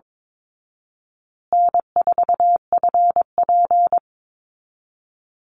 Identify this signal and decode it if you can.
Morse code, heard as a keyed tone, about 22 wpm, 700 Hz. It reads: N4FP
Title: Call Signs Repeated 2x - 22wpm